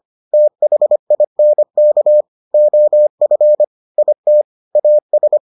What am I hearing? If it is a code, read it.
Morse code, heard as a keyed tone, about 25 wpm, 600 Hz. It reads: THINK OF IT AS